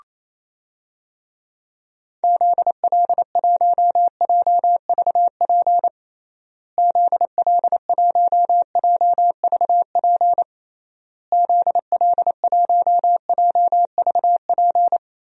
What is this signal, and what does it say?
Morse code, heard as a keyed tone, about 27 wpm, 700 Hz. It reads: ZL1JVP ZL1JVP ZL1JVP
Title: DXCC-100 Call Signs - 6 Characters - Repeated 3x - 28wpm